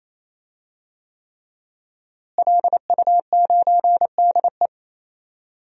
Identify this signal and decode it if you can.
Morse code, heard as a keyed tone, about 28 wpm, 700 Hz. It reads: LU9DE
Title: DXCC-100 Call Signs - 5 Characters - Repeated 3x - 28wpm